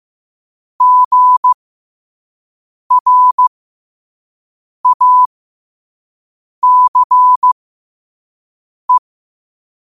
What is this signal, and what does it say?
Morse code, heard as a keyed tone, about 15 wpm, spaced as for 5 wpm, 1000 Hz. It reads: GRACE
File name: qotdfiles / 5wpm